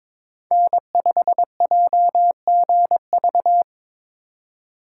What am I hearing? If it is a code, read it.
Morse code, heard as a keyed tone, about 22 wpm, 700 Hz. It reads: N5JGV
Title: USA Call Signs - 5 Characters - 4x Character Spacing - 22wpm